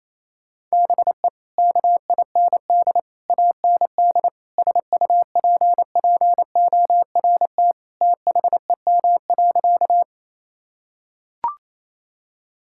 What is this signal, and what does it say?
Morse code, heard as a keyed tone, about 28 wpm, 700 Hz. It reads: BE KIND AND SUPPORT THEM.
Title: Sentences from Top 1000 Words - Review Part 5 - 28wpm